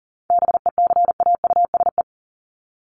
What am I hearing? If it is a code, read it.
Morse code, heard as a keyed tone, about 40 wpm, 700 Hz. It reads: BECAUSE
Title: Top 501-750 Words 40wpm